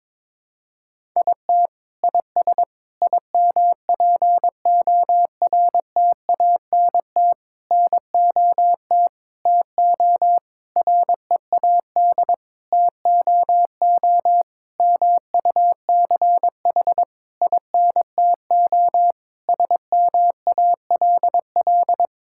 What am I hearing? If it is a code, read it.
Morse code, heard as a keyed tone, about 22 wpm, 700 Hz. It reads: IT IS IMPORTANT NOT TO READ TOO MUCH INTO SMALL
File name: Sentences with 9-Word Phrases - 22wpm